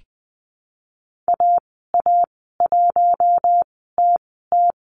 Morse code, heard as a keyed tone, about 20 wpm, spaced as for 15 wpm, 700 Hz. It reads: AA1TT
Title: 042 Lesson 8g - Number 1 - Call Signs 20wpm